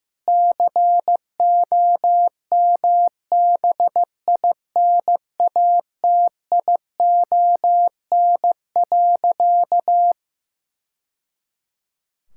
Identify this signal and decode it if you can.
Morse code, heard as a keyed tone, about 15 wpm, 700 Hz. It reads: COMBINATION.
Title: Sentences from Top 1000 Words - Review Part 6 - 15wpm